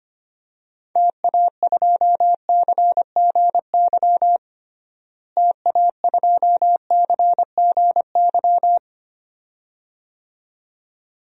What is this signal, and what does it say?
Morse code, heard as a keyed tone, about 25 wpm, 700 Hz. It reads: TA2CGY TA2CGY
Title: DXCC-100 Call Signs - 6 Characters - Repeated 2x - 25wpm